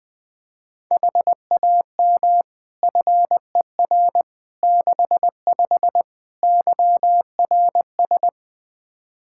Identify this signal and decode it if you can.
Morse code, heard as a keyed tone, about 20 wpm, 700 Hz. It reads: HAM FER 65 YRS
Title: Practice QSO - Exchange 3 - More Info - 20wpm